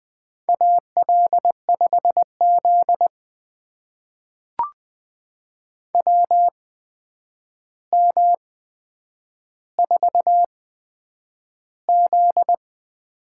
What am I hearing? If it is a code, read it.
Morse code, heard as a keyed tone, about 20 wpm, spaced as for 9 wpm, 700 Hz. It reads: AL5Z WM4Z
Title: USA Call Signs - 4 Characters - 8x Character Spacing - 20wpm